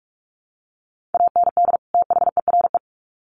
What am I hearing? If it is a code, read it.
Morse code, heard as a keyed tone, about 44 wpm, 700 Hz. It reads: AND THERE
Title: Sets of 2 Words 45wpm